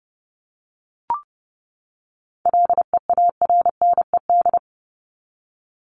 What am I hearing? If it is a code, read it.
Morse code, heard as a keyed tone, about 30 wpm, 700 Hz. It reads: LEARNED
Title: ED Suffix - Top 100 Words with Base Words 30wpm